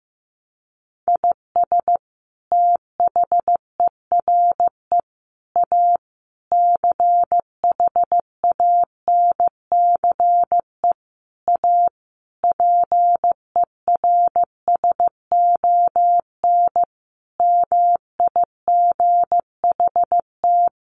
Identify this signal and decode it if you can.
Morse code, heard as a keyed tone, about 15 wpm, 700 Hz. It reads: IS THERE A CHANCE A PERSON MIGHT
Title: Sentences from Top 900 Words 15wpm